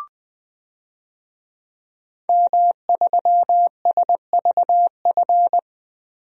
Morse code, heard as a keyed tone, about 20 wpm, 700 Hz. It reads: M3SVF
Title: DXCC-100 Call Signs - 5 Characters - Repeated 2x - 20wpm